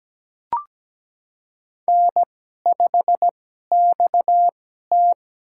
Morse code, heard as a keyed tone, about 17 wpm, spaced as for 12 wpm, 700 Hz. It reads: N5XT
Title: USA Call Signs - 2x Character Spacing - 17wpm